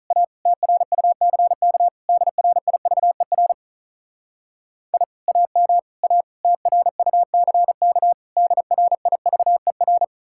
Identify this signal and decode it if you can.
Morse code, heard as a keyed tone, about 46 wpm, 700 Hz. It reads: A TRUCK DRIVER I AM A TRUCK DRIVER